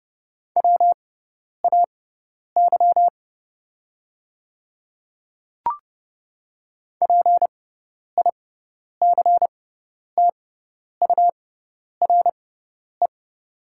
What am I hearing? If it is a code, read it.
Morse code, heard as a keyed tone, about 30 wpm, spaced as for 10 wpm, 700 Hz. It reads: WAY PICTURE
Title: 6x Character Spacing - Top 200 Words - 30wpm